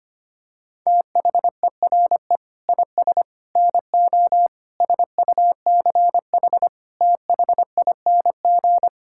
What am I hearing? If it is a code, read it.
Morse code, heard as a keyed tone, about 25 wpm, 700 Hz. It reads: THERE IS NO SUCH THING
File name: Sets of 9 Words - 25wpm